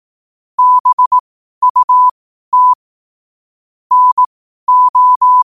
Morse code, heard as a keyed tone, about 18 wpm, spaced as for 12 wpm, 1000 Hz. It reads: BUT NO
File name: qotdfiles / 12wpm